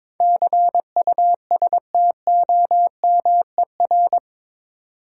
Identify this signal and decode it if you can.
Morse code, heard as a keyed tone, about 22 wpm, 700 Hz. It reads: CUSTOMER
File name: practice / 22wpm / Eight Letter Words - 22wpm